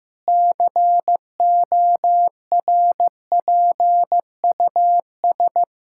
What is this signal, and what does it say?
Morse code, heard as a keyed tone, about 16 wpm, 700 Hz. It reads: CORPUS
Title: K1USN SST QSO Practice 15wpm